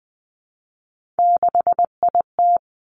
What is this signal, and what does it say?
Morse code, heard as a keyed tone, about 20 wpm, 700 Hz. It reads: 6IT
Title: Three Challenging Letter-Number Combinations 20wpm